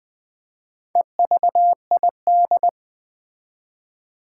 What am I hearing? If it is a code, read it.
Morse code, heard as a keyed tone, about 21 wpm, 700 Hz. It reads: EVID
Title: Build A Word - 4 Letters At A Time - 8-Letter Words - 20wpm